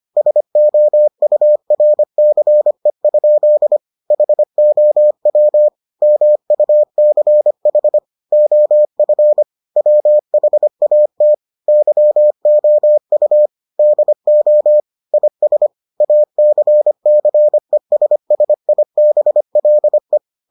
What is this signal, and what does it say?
Morse code, heard as a keyed tone, about 25 wpm, 600 Hz. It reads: SOURCE? HOW MUCH OF WHAT YOU DO IS ACCESSIBLE